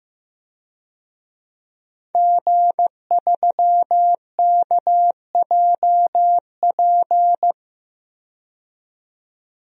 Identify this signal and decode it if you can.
Morse code, heard as a keyed tone, about 15 wpm, 700 Hz. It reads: G3KJP
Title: DXCC-100 Call Signs - 5 Characters - 15wpm